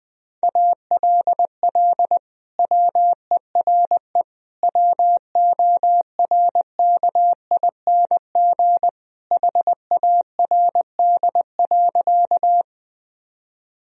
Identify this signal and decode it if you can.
Morse code, heard as a keyed tone, about 20 wpm, 700 Hz. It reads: ALL WERE WORKING HARD.
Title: Sentences from Top 1000 Words - Review Part 4 - 20wpm